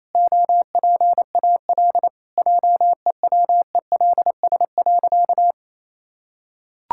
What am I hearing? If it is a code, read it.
Morse code, heard as a keyed tone, about 28 wpm, 700 Hz. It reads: OPAL JEWELS.
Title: Pangram Challenge v2 - 28wpm